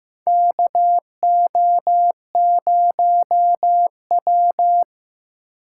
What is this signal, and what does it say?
Morse code, heard as a keyed tone, about 15 wpm, 700 Hz. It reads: KO0W
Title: USA Call Signs - 4 Characters - 8x Character Spacing - 15wpm